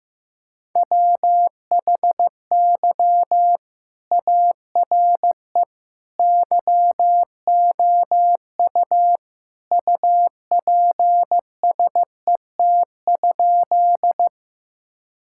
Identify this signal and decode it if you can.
Morse code, heard as a keyed tone, about 15 wpm, 700 Hz. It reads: WHY ARE YOU UPSET?
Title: Sentences from Top 1000 Words - Review Part 5 - 15wpm